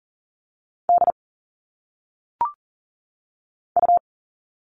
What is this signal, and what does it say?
Morse code, heard as a keyed tone, about 40 wpm, 700 Hz. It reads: D U
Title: All Cut Numbers - 1-Digit 40wpm